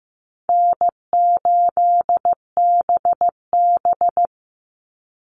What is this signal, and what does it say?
Morse code, heard as a keyed tone, about 15 wpm, 700 Hz. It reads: N8BB
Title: NA SOTA Chaser Callsigns 15wpm